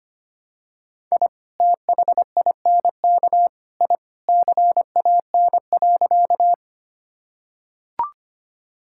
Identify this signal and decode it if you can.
Morse code, heard as a keyed tone, about 25 wpm, 700 Hz. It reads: I THINK I CAN.
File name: practice / 25wpm / Sentences with 3-Word Phrases - 25wpm